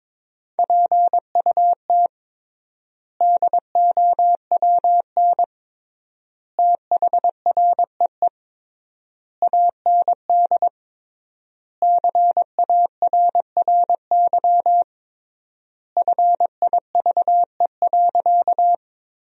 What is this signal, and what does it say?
Morse code, heard as a keyed tone, about 22 wpm, 700 Hz. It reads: PUT DOWN THREE AND CARRY FIVE.
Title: 3x Word-Spacing - Sentences from Top 400 Words - 22wpm